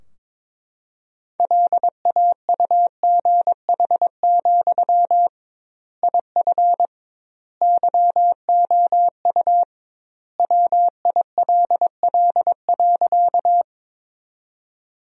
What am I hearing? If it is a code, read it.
Morse code, heard as a keyed tone, about 22 wpm, 700 Hz. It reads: LAUGH, IF YOU WILL.
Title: 2x Word-Spacing - Sentences from Top 500 Words - 22wpm